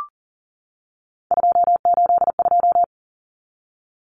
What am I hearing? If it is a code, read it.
Morse code, heard as a keyed tone, about 40 wpm, 700 Hz. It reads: 282
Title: Numbers - 3-Digit 40wpm